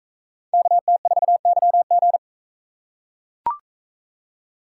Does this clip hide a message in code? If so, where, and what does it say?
Yes: Morse code from 0.5 to 2.2 s, heard as a keyed tone, about 42 wpm, 700 Hz. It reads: KT4YG